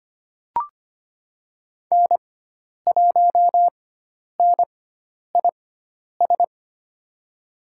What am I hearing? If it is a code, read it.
Morse code, heard as a keyed tone, about 25 wpm, spaced as for 10 wpm, 700 Hz. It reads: N1NIS
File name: Lesson 16g - Letter c - Call Signs 25wpm